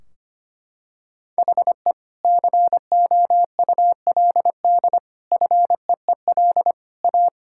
Morse code, heard as a keyed tone, about 25 wpm, 700 Hz. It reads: HE COULD FEEL A
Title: Sentences from Top 1600 Words 25wpm